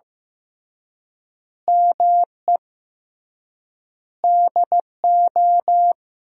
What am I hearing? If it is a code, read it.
Morse code, heard as a keyed tone, about 15 wpm, 700 Hz. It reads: ME DO
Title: 3x Word-Spacing - Sentences from Top 300 Words - 15wpm